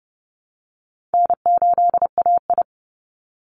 Morse code, heard as a keyed tone, about 28 wpm, 700 Hz. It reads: N8AI
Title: Easy Call Signs Repeated 3x - 30wpm